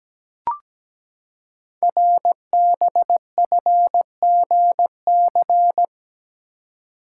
Morse code, heard as a keyed tone, about 17 wpm, 700 Hz. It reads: RBFGC
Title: Code Groups - 5 Letters - 17wpm